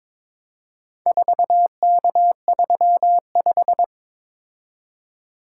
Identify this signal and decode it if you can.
Morse code, heard as a keyed tone, about 22 wpm, 700 Hz. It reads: 4K35
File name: practice / 22wpm / Four Letter-Number Combinations - 22wpm